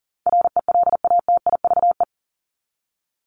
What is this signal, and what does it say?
Morse code, heard as a keyed tone, about 40 wpm, 700 Hz. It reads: RELATIVE
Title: LY Suffix - Top 100 Words with Base Words 40wpm